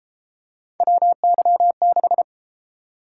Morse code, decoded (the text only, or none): WY6